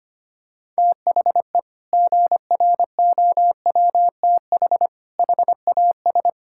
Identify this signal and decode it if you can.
Morse code, heard as a keyed tone, about 25 wpm, 700 Hz. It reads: THE GROWTH HAS
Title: Sentences from Top 1000 Words - Review Part 5 - 25wpm